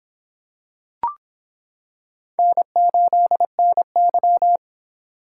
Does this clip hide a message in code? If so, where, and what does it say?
Yes: Morse code from 2.4 to 4.6 s, heard as a keyed tone, about 25 wpm, 700 Hz. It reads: N8NY